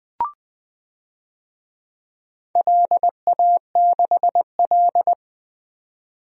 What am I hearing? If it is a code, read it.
Morse code, heard as a keyed tone, about 20 wpm, 700 Hz. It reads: LA6L
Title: DXCC-100 Call Signs - 4 Characters - 20wpm